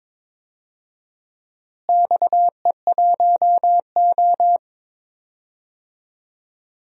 Morse code, heard as a keyed tone, about 21 wpm, 700 Hz. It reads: XE1O
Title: DXCC-100 Call Signs - 4 Characters - Repeated 2x - 22wpm